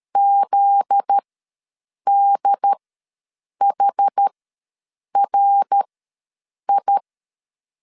Morse code, heard as a keyed tone, about 13 wpm, spaced as for 7 wpm, 800 Hz. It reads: ZDHRI
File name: LEZIONI / 10 SIGNS / 10D COMA ws 12wpm 3618